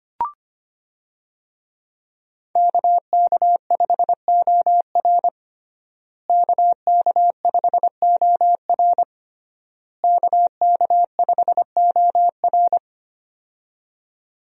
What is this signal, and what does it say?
Morse code, heard as a keyed tone, about 25 wpm, 700 Hz. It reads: KK5OR KK5OR KK5OR
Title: USA Call Signs - 5 Characters - Repeated 3x - 25wpm